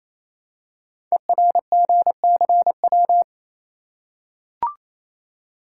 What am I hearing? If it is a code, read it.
Morse code, heard as a keyed tone, about 28 wpm, 700 Hz. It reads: ERGCW
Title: Code Groups - 5 Letter:Numbers - 28wpm